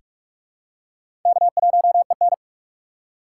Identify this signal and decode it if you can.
Morse code, heard as a keyed tone, about 46 wpm, 700 Hz. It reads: K1EN
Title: USA Call Signs - 4 Characters - 3x Character Spacing - 45wpm